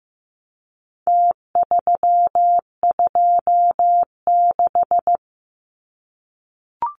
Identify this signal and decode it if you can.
Morse code, heard as a keyed tone, about 15 wpm, 700 Hz. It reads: T326
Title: Numbers - Contest Style - 4-Digit 15wpm